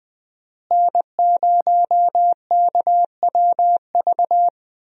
Morse code, heard as a keyed tone, about 20 wpm, 700 Hz. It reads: N0KWV
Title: USA Call Signs - 5 Characters - Repeated 3x - 20wpm